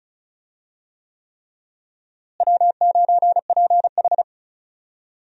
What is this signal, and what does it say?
Morse code, heard as a keyed tone, about 34 wpm, 700 Hz. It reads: W9PH
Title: USA Call Signs - 4 Characters - Repeated 3x - 35wpm